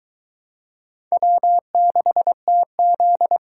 Morse code, heard as a keyed tone, about 22 wpm, 700 Hz. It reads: W6TZ